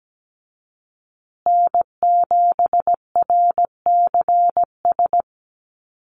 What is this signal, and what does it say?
Morse code, heard as a keyed tone, about 17 wpm, 700 Hz. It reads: N7RCS
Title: Call Signs Repeated 3x - 17wpm